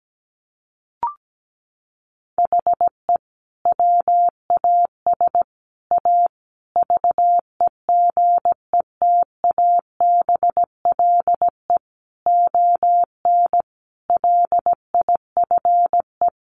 Morse code, heard as a keyed tone, about 17 wpm, 700 Hz. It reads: HE WAS A VEGETABLE ON LIFE
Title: Sentences from Top 1300 Words 17wpm